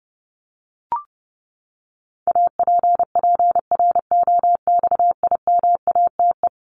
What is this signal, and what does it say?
Morse code, heard as a keyed tone, about 30 wpm, 700 Hz. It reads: APPROXIMATE
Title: LY Suffix - Top 100 Words with Base Words 30wpm